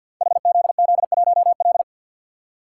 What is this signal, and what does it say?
Morse code, heard as a keyed tone, about 51 wpm, 700 Hz. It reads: HZZ1L
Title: Code Groups - 5 Letter:Numbers - 50wpm